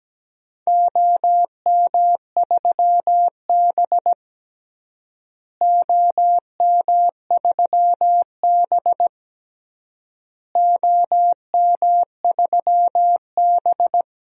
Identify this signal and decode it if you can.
Morse code, heard as a keyed tone, about 17 wpm, 700 Hz. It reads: OM3B OM3B OM3B
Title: DXCC-100 Call Signs - 4 Characters - Repeated 3x - 17wpm